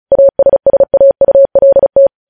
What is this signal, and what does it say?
Morse code, heard as a keyed tone, about 36 wpm, 550 Hz. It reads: ASSAULT
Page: news headlines in morse code at 35 wpm